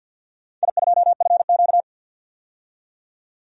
Morse code, heard as a keyed tone, about 53 wpm, 700 Hz. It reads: I2FQ
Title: DXCC-100 Call Signs - 4 Characters - 50wpm